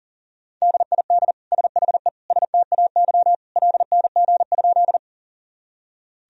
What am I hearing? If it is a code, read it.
Morse code, heard as a keyed tone, about 40 wpm, 700 Hz. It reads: DID SHE STAY LNG?
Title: Sentences from Top 500 Words and Abbreviations 40wpm